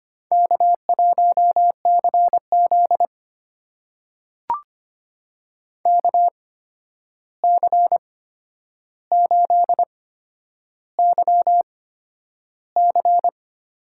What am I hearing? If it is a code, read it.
Morse code, heard as a keyed tone, about 25 wpm, spaced as for 10 wpm, 700 Hz. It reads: K1CZ KC8YC
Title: USA Call Signs - 8x Character Spacing - 25wpm